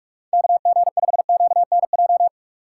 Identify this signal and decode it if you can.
Morse code, heard as a keyed tone, about 42 wpm, 700 Hz. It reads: KG5QNJ
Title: USA Call Signs - 3x Character Spacing - 45wpm